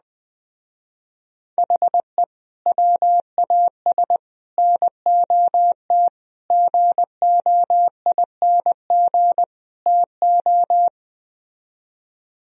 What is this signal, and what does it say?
Morse code, heard as a keyed tone, about 21 wpm, 700 Hz. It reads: HE WAS NOT GOING TO